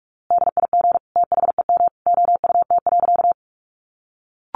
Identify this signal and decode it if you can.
Morse code, heard as a keyed tone, about 46 wpm, 700 Hz. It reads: DIG THEM OUT.
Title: A Sound Of Thunder 45wpm